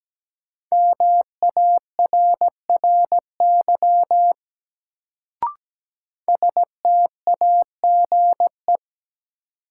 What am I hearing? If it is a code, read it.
Morse code, heard as a keyed tone, about 17 wpm, 700 Hz. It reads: MARRY STAGE
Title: Sentences from Top 1200 Words - New Word Drill - 17wpm